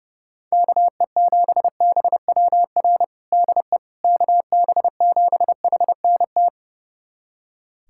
Morse code, heard as a keyed tone, about 30 wpm, 700 Hz. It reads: KE7BWR DE KB7HNT
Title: USA Callsign DE Callsign - 6 Characters - 30wpm